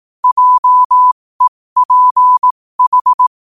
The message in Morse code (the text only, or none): JEPH